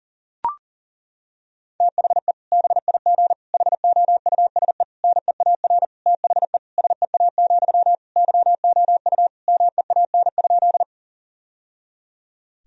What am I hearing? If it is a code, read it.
Morse code, heard as a keyed tone, about 37 wpm, 700 Hz. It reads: THE BIG HOUSE NEAR THE SEA, YOU MEAN?
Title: Sentences from Top 1000 Words - Review Part 1 - 40wpm